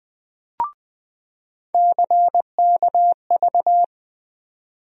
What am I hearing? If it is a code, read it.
Morse code, heard as a keyed tone, about 20 wpm, 700 Hz. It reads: CKV
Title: USA Call Sign Postfixes - 3 Letters 20wpm